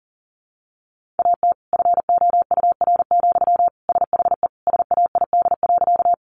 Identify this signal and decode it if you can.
Morse code, heard as a keyed tone, about 40 wpm, 700 Hz. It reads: AT FOUR, SHE SAID.
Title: Sentences from Top 300 Words 40wpm